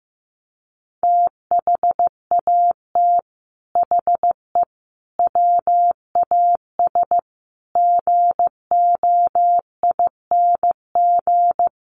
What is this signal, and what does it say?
Morse code, heard as a keyed tone, about 15 wpm, 700 Hz. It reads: THAT HE WAS GOING
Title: Sets of 5 Words 15wpm